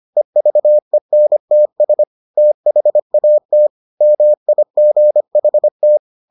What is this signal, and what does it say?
Morse code, heard as a keyed tone, about 25 wpm, 600 Hz. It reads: EVENTS THAT MIGHT